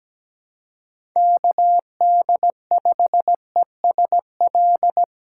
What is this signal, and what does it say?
Morse code, heard as a keyed tone, about 17 wpm, 700 Hz. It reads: KD5ESL
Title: USA Call Signs - 6x Character Spacing - 17wpm